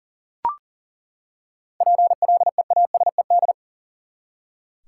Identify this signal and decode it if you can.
Morse code, heard as a keyed tone, about 40 wpm, 700 Hz. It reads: PLEASED
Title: Seven Letter Words - 40wpm